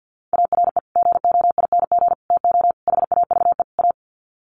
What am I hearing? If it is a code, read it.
Morse code, heard as a keyed tone, about 50 wpm, 700 Hz. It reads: ARE GOING TO HAVE A
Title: Sets of 5 Words Encore2 50wpm